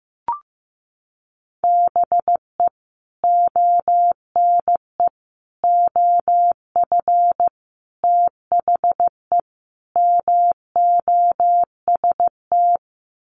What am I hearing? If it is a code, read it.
Morse code, heard as a keyed tone, about 15 wpm, 700 Hz. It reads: BE ONE OF THE MOST
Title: Sets of 5 Words Encore 15wpm